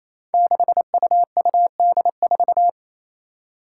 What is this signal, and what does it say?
Morse code, heard as a keyed tone, about 27 wpm, 700 Hz. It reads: 6UUD4
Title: Five Challenging Letter-Number Combinations 28wpm